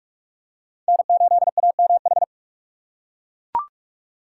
Morse code, heard as a keyed tone, about 41 wpm, 690 Hz. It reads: N8AMH
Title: USA Call Signs - 5 Characters - 2x Character Spacing - 45wpm